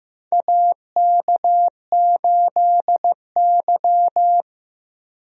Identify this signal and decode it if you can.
Morse code, heard as a keyed tone, about 15 wpm, 700 Hz. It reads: AK8Y
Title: USA Call Signs - 4 Characters - Repeated 2x - 15wpm